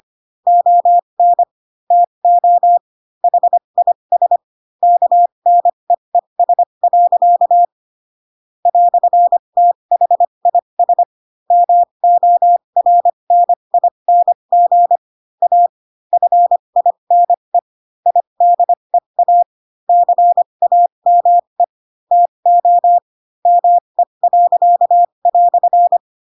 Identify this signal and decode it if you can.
Morse code, heard as a keyed tone, about 25 wpm, spaced as for 22 wpm, 700 Hz. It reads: ON TO HIS KNEES. "THIS MORNING A FINE IDEA CAME TO ME."
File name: Pinocchio / Pinocchio ch2 18 25 ik1hge